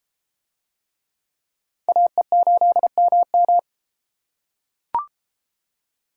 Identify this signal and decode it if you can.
Morse code, heard as a keyed tone, about 31 wpm, 700 Hz. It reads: AE8MM